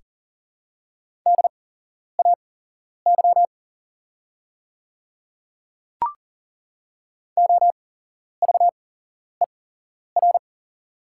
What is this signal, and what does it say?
Morse code, heard as a keyed tone, about 40 wpm, spaced as for 11 wpm, 690 Hz. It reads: DAY OVER
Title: 8x Character Spacing - Top 100 Words - 40wpm